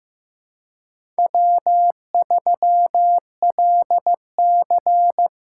Morse code, heard as a keyed tone, about 15 wpm, 700 Hz. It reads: W3LC
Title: USA Call Signs - 4x Character Spacing - 15wpm